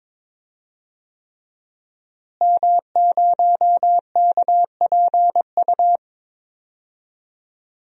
Morse code, heard as a keyed tone, about 22 wpm, 700 Hz. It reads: M0KPU
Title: DXCC-100 Call Signs - 5 Characters - Repeated 3x - 22wpm